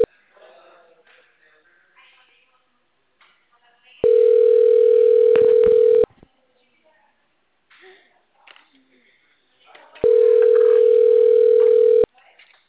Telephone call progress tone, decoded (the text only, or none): ringback tone